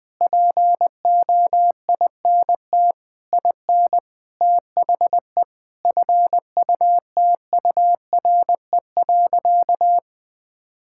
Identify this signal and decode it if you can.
Morse code, heard as a keyed tone, about 20 wpm, 700 Hz. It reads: POINT IN THE FUTURE.
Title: Sentences with 6-Word Phrases - 20wpm